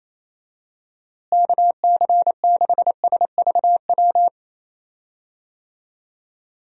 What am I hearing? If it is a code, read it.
Morse code, heard as a keyed tone, about 27 wpm, 700 Hz. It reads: KC6SVW
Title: USA Call Signs - 6 Characters - Repeated 2x - 28wpm